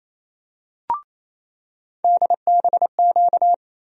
Morse code, heard as a keyed tone, about 28 wpm, 700 Hz. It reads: DBQ
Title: USA Call Sign Postfixes - 3 Letters 28wpm